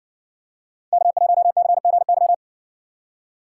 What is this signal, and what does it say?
Morse code, heard as a keyed tone, about 56 wpm, 700 Hz. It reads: K1CGY